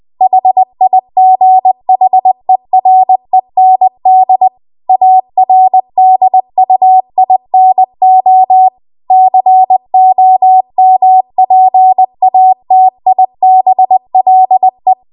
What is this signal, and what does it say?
Morse code, heard as a keyed tone, about 20 wpm, 750 Hz. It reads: HIGHEREND ARDUINO COMPATIBLE